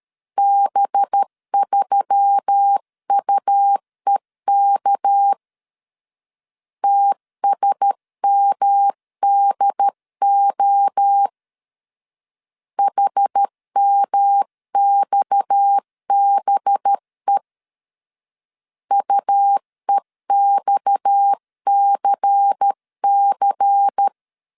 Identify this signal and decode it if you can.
Morse code, heard as a keyed tone, about 13 wpm, 800 Hz. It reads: B3UEK TSMDO HMXBE UEXCC
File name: CW / DET / 1 6 1500C S C 12wpm 3533